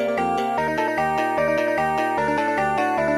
There is no code.